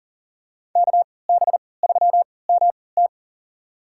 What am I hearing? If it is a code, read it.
Morse code, heard as a keyed tone, about 40 wpm, spaced as for 23 wpm, 700 Hz. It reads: KB3MT
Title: USA Call Signs - 3x Character Spacing - 40wpm